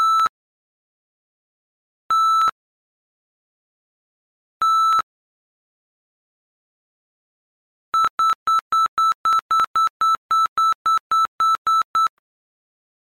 There is no code.